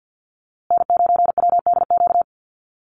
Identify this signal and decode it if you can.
Morse code, heard as a keyed tone, about 56 wpm, 700 Hz. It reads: N9WDQ
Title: USA Call Signs Repeated 2x 50wpm